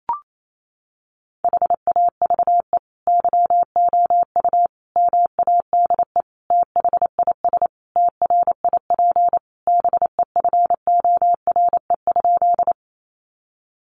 Morse code, heard as a keyed tone, about 28 wpm, 700 Hz. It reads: HAVE YOU MADE THIS TRIP BEFORE?
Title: Sentences from Top 1300 Words 28wpm